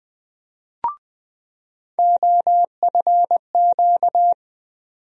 Morse code, heard as a keyed tone, about 20 wpm, 700 Hz. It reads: OFQ
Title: USA Call Sign Postfixes - 3 Letters 20wpm